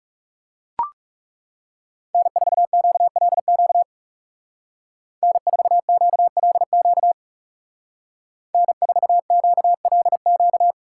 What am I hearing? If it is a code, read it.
Morse code, heard as a keyed tone, about 35 wpm, 700 Hz. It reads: N4QLQ N4QLQ N4QLQ
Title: Speed Racing - USA 1x3 Call Signs 30wpm